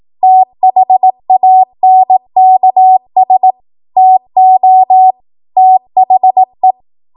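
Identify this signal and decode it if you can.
Morse code, heard as a keyed tone, about 18 wpm, 750 Hz. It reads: THANKS TO THE
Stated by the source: ARRL 18 wpm Morse code practice archive transmission.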